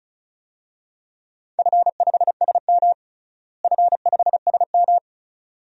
Morse code, heard as a keyed tone, about 34 wpm, 700 Hz. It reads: F5SM F5SM
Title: DXCC-100 Call Signs - 4 Characters - Repeated 2x - 35wpm